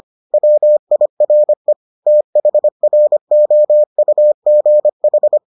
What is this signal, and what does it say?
Morse code, heard as a keyed tone, about 25 wpm, 600 Hz. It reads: WIRE THROUGH